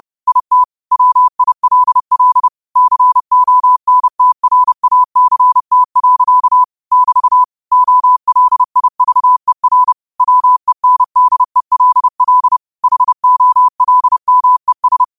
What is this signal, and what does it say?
Morse code, heard as a keyed tone, about 30 wpm, 1000 Hz. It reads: IT WILL CONTRACT. <BT> OLIVER WENDELL HOLMES